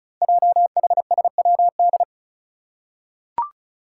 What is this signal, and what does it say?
Morse code, heard as a keyed tone, about 36 wpm, 690 Hz. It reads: JHSWD
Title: Code Groups - 5 Letters - 35wpm